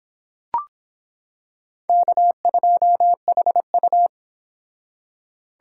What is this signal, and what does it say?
Morse code, heard as a keyed tone, about 26 wpm, 700 Hz. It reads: K2HU